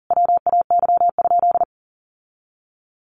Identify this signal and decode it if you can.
Morse code, heard as a keyed tone, about 40 wpm, 700 Hz. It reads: WAY?
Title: Sentences from Top 1100 Words 40wpm